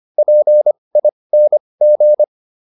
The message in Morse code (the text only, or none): PING